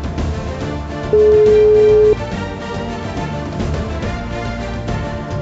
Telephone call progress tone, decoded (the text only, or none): ringback tone